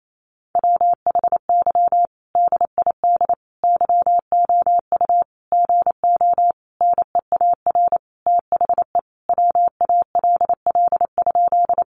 Morse code, heard as a keyed tone, about 28 wpm, 700 Hz. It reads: WHY DID YOU GO NEAR THE WALL?
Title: Sentences from Top 1100 Words 28wpm